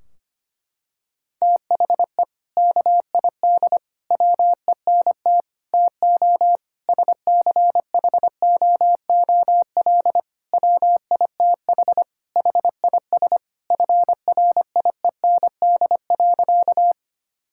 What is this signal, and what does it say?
Morse code, heard as a keyed tone, about 25 wpm, 700 Hz. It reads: THE KID WENT TO SCHOOL WITH HIS FRIEND.
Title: Sentences from Top 1600 Words 25wpm